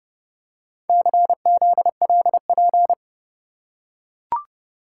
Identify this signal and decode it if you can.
Morse code, heard as a keyed tone, about 28 wpm, 700 Hz. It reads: CZLP